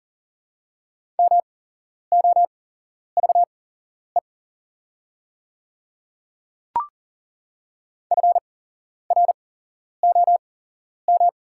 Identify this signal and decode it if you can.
Morse code, heard as a keyed tone, about 40 wpm, spaced as for 11 wpm, 700 Hz. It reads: MOVE FROM
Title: 8x Character Spacing - Top 500 Words - 40wpm